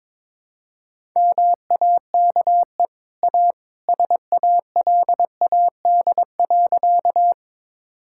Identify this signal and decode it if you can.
Morse code, heard as a keyed tone, about 22 wpm, 690 Hz. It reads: MAKE A SALAD.
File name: practice / 22wpm / Easy Sentences with Hints - 22wpm